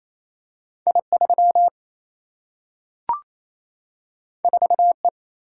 Morse code, heard as a keyed tone, about 28 wpm, 700 Hz. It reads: I 3 4 E
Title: Single-Letter Single-Number 28wpm